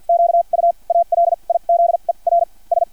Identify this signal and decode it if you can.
Morse code, heard as a keyed tone, about 50 wpm, spaced as for 35 wpm, 680 Hz. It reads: QUALIZERS
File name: archived / CRSnet